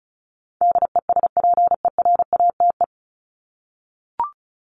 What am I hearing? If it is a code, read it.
Morse code, heard as a keyed tone, about 34 wpm, 700 Hz. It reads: DESPERATE
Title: Top 1751-2000 Words 35wpm